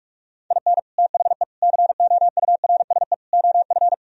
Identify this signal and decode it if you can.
Morse code, heard as a keyed tone, about 48 wpm, 700 Hz. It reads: IN THE COURSE OF